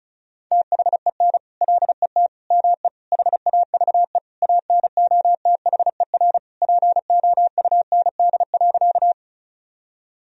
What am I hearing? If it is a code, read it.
Morse code, heard as a keyed tone, about 36 wpm, 700 Hz. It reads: THEN LET ME HAVE ANOTHER POUND.
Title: Sentences from Top 1000 Words - Review Part 2 - 35wpm